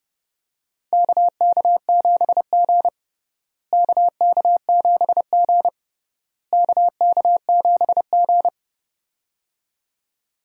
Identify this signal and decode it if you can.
Morse code, heard as a keyed tone, about 30 wpm, 700 Hz. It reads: KK7G KK7G KK7G
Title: USA Call Signs - 4 Characters - Repeated 3x - 30wpm